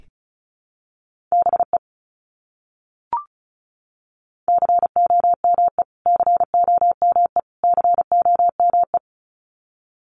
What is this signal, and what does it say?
Morse code, heard as a keyed tone, about 34 wpm, 700 Hz. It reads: BE COME COME COME
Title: Top 200 Words Repeated 35wpm